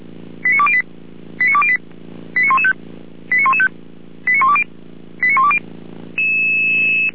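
Five-tone selective call (ZVEI), heard as quick runs of five tones, tones of about 70 ms. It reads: 89208 89208 89115 89115 89120 89120
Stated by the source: ZVEI five-tone selective call received on SDR.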